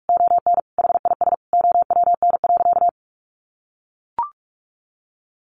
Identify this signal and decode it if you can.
Morse code, heard as a keyed tone, about 44 wpm, 700 Hz. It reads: ON HIS OWN.
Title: Sentences from Top 1400 Words 45wpm